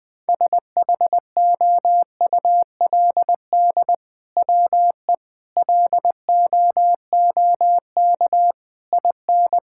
Morse code, heard as a keyed tone, about 20 wpm, 700 Hz. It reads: SHOULD WE LOOK IN
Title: Sentences from Top 1600 Words 20wpm